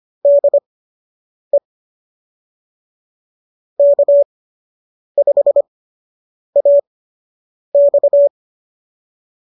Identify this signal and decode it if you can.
Morse code, heard as a keyed tone, about 25 wpm, spaced as for 8 wpm, 590 Hz. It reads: DE K5AX